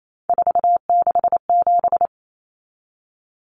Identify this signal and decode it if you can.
Morse code, heard as a keyed tone, about 28 wpm, 700 Hz. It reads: 467
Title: Numbers - 3-Digit 28wpm